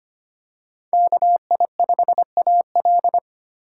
Morse code, heard as a keyed tone, about 25 wpm, 700 Hz. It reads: KI5AL
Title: K1USN SST QSO Practice - 25wpm